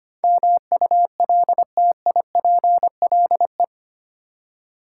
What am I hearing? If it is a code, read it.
Morse code, heard as a keyed tone, about 25 wpm, 700 Hz. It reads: MULTIPLE
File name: Eight Letter Words - 25wpm